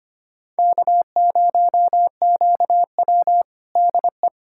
Morse code, heard as a keyed tone, about 25 wpm, 700 Hz. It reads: K0QW DE
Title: USA Callsign DE Callsign - 4 Characters - 25wpm